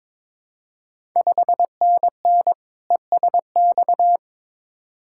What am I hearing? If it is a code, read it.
Morse code, heard as a keyed tone, about 22 wpm, 700 Hz. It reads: 5NN ESX
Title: Ontario QSO Party Exchange 22wpm